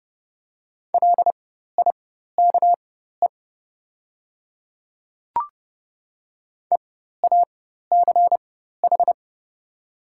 Morse code, heard as a keyed tone, about 30 wpm, spaced as for 14 wpm, 700 Hz. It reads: LIKE EACH
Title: 4x Character Spacing - Top 100 Words - 30wpm